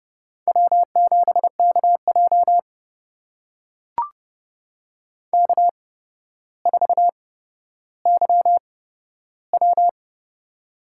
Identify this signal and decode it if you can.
Morse code, heard as a keyed tone, about 30 wpm, spaced as for 13 wpm, 700 Hz. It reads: W7KJ K4YW
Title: USA Call Signs - 4 Characters - 8x Character Spacing - 30wpm